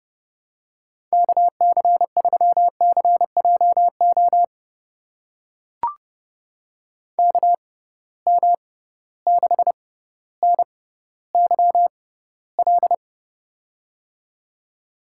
Morse code, heard as a keyed tone, about 30 wpm, spaced as for 15 wpm, 700 Hz. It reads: KC3CJO KM6NYL
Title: USA Call Signs - 6 Characters - 6x Character Spacing - 30wpm